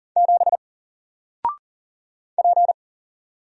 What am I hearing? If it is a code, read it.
Morse code, heard as a keyed tone, about 39 wpm, 700 Hz. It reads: 7 P
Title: Single Character (Letter:Number:Punctuation:Prosign) 40wpm